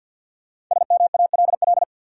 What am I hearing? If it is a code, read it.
Morse code, heard as a keyed tone, about 48 wpm, 700 Hz. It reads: SMALL
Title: Sentences from Top 200 Words - New Word Drill - 50wpm